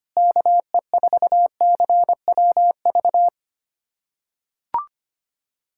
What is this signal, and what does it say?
Morse code, heard as a keyed tone, about 25 wpm, 700 Hz. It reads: KE4CWV
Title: USA Callsign DE Callsign 25wpm